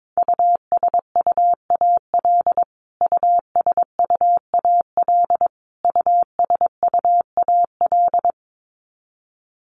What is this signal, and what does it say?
Morse code, heard as a keyed tone, about 22 wpm, 700 Hz. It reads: USUAL USUAL USUAL
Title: Top 500 Words Repeated 22wpm